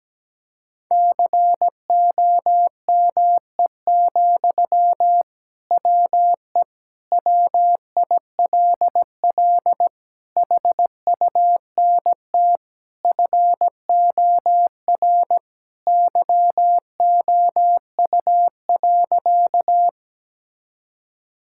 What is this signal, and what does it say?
Morse code, heard as a keyed tone, about 17 wpm, 700 Hz. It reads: COME, WE WILL HUNT FOR YOU.
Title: Sentences from Top 1600 Words 17wpm